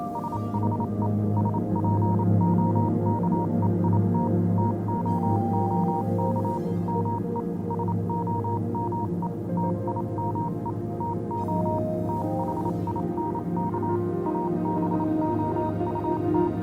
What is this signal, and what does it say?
Morse code, heard as a keyed tone, about 27 wpm, 940 Hz, with noise. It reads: SHE SPOKE IT TOOK ME SOME TIME TO THINK ABOUT